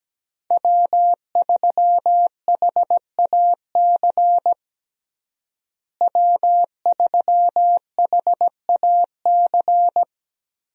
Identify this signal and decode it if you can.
Morse code, heard as a keyed tone, about 17 wpm, 700 Hz. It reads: W3HAC W3HAC
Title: USA Call Signs - 5 Characters - Repeated 3x - 17wpm